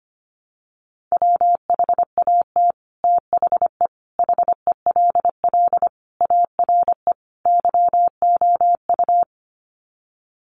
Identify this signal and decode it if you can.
Morse code, heard as a keyed tone, about 25 wpm, 700 Hz. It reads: WHAT THE HELL ARE YOU
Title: Sets of 5 Words Encore 25wpm